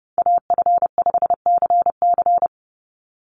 Morse code, heard as a keyed tone, about 31 wpm, 700 Hz. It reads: AF5CC
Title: Call Signs - 30wpm